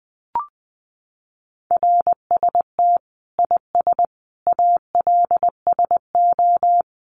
Morse code, heard as a keyed tone, about 20 wpm, 700 Hz. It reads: RST IS ALSO
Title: Common QSO Phrases 20wpm